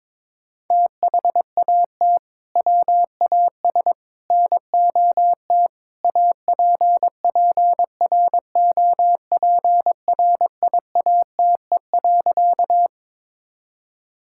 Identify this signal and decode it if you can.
Morse code, heard as a keyed tone, about 21 wpm, 700 Hz. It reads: THAT WAS NOT APPROPRIATE.
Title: Sentences from Top 1000 Words - Review Part 5 - 22wpm